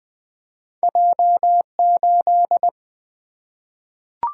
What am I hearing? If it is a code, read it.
Morse code, heard as a keyed tone, about 20 wpm, 700 Hz. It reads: J 8
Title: Single-Letter Single-Number 20wpm